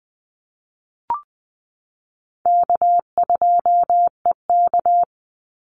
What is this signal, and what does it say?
Morse code, heard as a keyed tone, about 20 wpm, 700 Hz. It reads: K2EK
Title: Easy Call Signs Repeated 2x - 20wpm